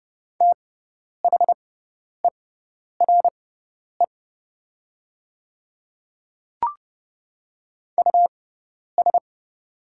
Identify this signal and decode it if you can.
Morse code, heard as a keyed tone, about 30 wpm, spaced as for 10 wpm, 700 Hz. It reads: THERE US
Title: 6x Character Spacing - Top 200 Words - 30wpm